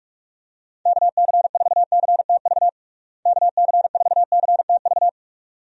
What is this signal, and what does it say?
Morse code, heard as a keyed tone, about 41 wpm, 700 Hz. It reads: KC4CTV KC4CTV
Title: USA Call Signs - 6 Characters - Repeated 2x - 45wpm